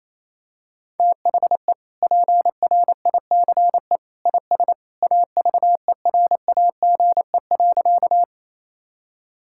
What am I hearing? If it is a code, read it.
Morse code, heard as a keyed tone, about 28 wpm, 700 Hz. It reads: THE PRICE IS AVERAGE.
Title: Sentences from Top 1000 Words - Review Part 6 - 28wpm